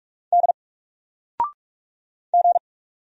Morse code, heard as a keyed tone, about 46 wpm, 700 Hz. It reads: D G
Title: Single Character (Letter:Number:Punctuation:Prosign) 45wpm